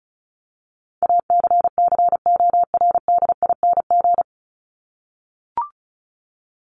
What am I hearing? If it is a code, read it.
Morse code, heard as a keyed tone, about 34 wpm, 700 Hz. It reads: ACCORDING
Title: LY Suffix - Top 100 Words with Base Words 35wpm